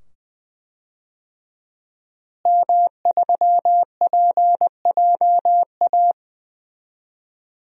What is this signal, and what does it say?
Morse code, heard as a keyed tone, about 20 wpm, 700 Hz. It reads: M3PJA
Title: DXCC-100 Call Signs - 5 Characters - Repeated 2x - 20wpm